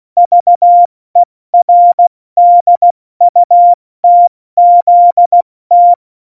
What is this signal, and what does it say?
Morse code, heard as a keyed tone, about 16 wpm, 700 Hz. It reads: VERDUTZT
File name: Morsekurs / Witze / Witz 10.Dezember 2019, 70 F80